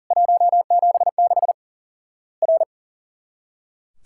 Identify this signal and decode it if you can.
Morse code, heard as a keyed tone, about 40 wpm, 700 Hz. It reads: 176 R
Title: Worked All Europe DX Contest - QTC Exchange - 40wpm